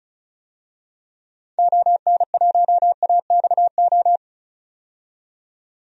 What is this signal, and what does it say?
Morse code, heard as a keyed tone, about 36 wpm, 700 Hz. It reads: ON1AXO
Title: DXCC-100 Call Signs - 6 Characters - 35wpm